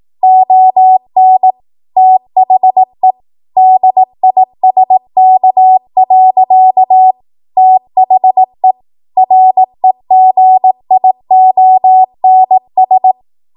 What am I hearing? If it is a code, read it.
Morse code, heard as a keyed tone, about 18 wpm, 750 Hz. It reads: ON THE DISK. THE REGIONS
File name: Morse / 18WPM